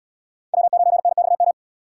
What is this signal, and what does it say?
Morse code, heard as a keyed tone, about 77 wpm, 700 Hz. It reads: W9NXM